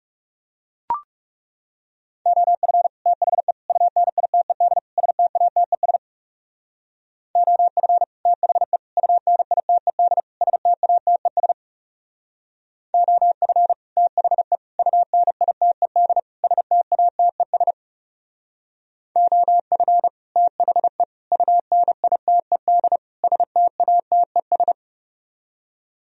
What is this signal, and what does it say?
Morse code, heard as a keyed tone, about 37 wpm, 700 Hz. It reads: OF THE UNITED STATES OF THE UNITED STATES OF THE UNITED STATES OF THE UNITED STATES
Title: Speed-Racing - Sets of 4 Words 30wpm